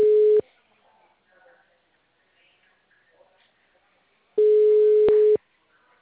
Ringback tone (Europe, 425 Hz).
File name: out- -2011- - - .81